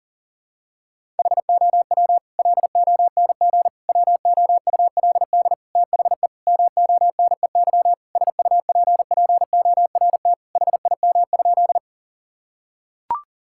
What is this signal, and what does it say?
Morse code, heard as a keyed tone, about 40 wpm, 700 Hz. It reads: HOW LONG WOULD THE MONEY SUPPORT HIM?
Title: Sentences from Top 1000 Words - Review Part 5 - 40wpm